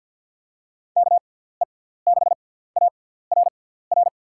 Morse code, heard as a keyed tone, about 46 wpm, spaced as for 17 wpm, 700 Hz. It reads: KE6ARR
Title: USA Call Signs - 6 Characters - 6x Character Spacing - 50wpm